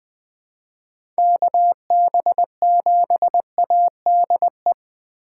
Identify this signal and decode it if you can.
Morse code, heard as a keyed tone, about 19 wpm, 700 Hz. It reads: KB7ADE
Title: USA Call Signs - 2x Character Spacing - 20wpm